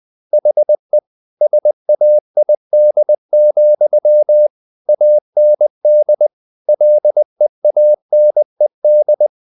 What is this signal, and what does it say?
Morse code, heard as a keyed tone, about 20 wpm, 600 Hz. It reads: HE SAID, AND LEANED